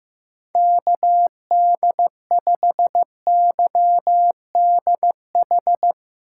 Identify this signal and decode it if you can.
Morse code, heard as a keyed tone, about 15 wpm, 700 Hz. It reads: KD5YDH
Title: USA Call Signs - 8x Character Spacing - 15wpm